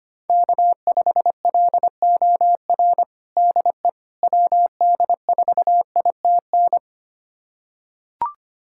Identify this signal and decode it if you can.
Morse code, heard as a keyed tone, about 25 wpm, 700 Hz. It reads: K5LOR DE WD4ITN
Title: USA Callsign DE Callsign 25wpm